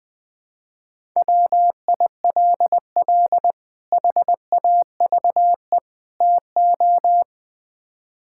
Morse code, heard as a keyed tone, about 20 wpm, 700 Hz. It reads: WILL HAVE TO
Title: Build A Phrase - 3-Word Phrases - 20wpm